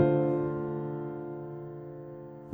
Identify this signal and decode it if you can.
Dial tone (North America), 350 + 440 Hz.